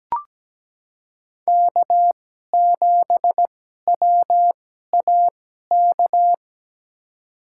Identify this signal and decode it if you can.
Morse code, heard as a keyed tone, about 17 wpm, spaced as for 12 wpm, 700 Hz. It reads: K7WAK
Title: USA Call Signs - 5 Characters - 2x Character Spacing - 17wpm